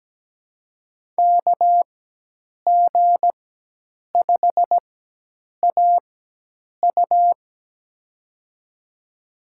Morse code, heard as a keyed tone, about 17 wpm, spaced as for 8 wpm, 700 Hz. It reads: KG5AU
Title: USA Call Signs - 5 Characters - 4x Character Spacing - 17wpm